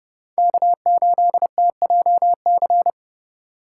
Morse code, heard as a keyed tone, about 36 wpm, 700 Hz. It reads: K8TJC
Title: Speed Racing - USA Call Signs 30wpm